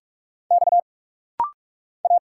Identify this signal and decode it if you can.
Morse code, heard as a keyed tone, about 46 wpm, 690 Hz. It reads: X A
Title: Single Letters - Rapid-Fire 45wpm